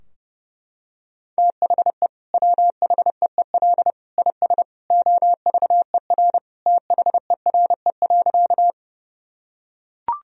Morse code, heard as a keyed tone, about 30 wpm, 700 Hz. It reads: THE WHEEL IS OVER THERE.
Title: Sentences from Top 1000 Words - Review Part 3 - 30wpm